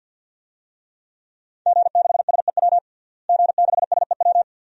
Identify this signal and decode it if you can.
Morse code, heard as a keyed tone, about 53 wpm, 700 Hz. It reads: G6SEW G6SEW
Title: DXCC-100 Call Signs - 5 Characters - Repeated 2x - 50wpm